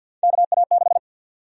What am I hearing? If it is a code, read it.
Morse code, heard as a keyed tone, about 53 wpm, 700 Hz. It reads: KA6
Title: USA Call Sign Prefixes - 50wpm